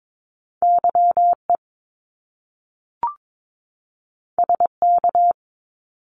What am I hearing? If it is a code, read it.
Morse code, heard as a keyed tone, about 22 wpm, 700 Hz. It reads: YE SK
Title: Top Two Letter Combinations in Words 22wpm